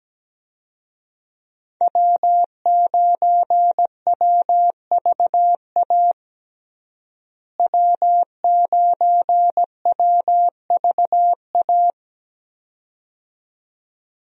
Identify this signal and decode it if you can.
Morse code, heard as a keyed tone, about 17 wpm, 700 Hz. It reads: W9WVA W9WVA
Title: USA Call Signs - 5 Characters - Repeated 2x - 17wpm